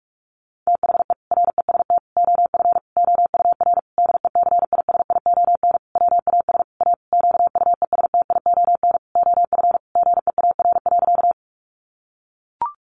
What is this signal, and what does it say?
Morse code, heard as a keyed tone, about 44 wpm, 700 Hz. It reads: THE REST OF OUR DECISION WAS A QUESTION OF GEAR.
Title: Sentences from Top 1400 Words 45wpm